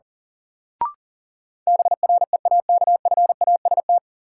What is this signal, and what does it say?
Morse code, heard as a keyed tone, about 37 wpm, 700 Hz. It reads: BREAKFAST
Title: Sentences from Top 1500 Words - New Word Drill Alt - 40wpm